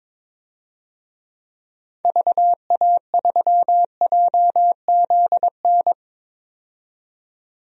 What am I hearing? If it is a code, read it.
Morse code, heard as a keyed tone, about 22 wpm, 700 Hz. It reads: VA3JZN
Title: DXCC-100 Call Signs - 6 Characters - 22wpm